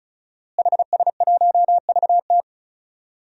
Morse code, heard as a keyed tone, about 35 wpm, 700 Hz. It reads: HS1VT
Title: DXCC-100 Call Signs - 5 Characters - Repeated 3x - 35wpm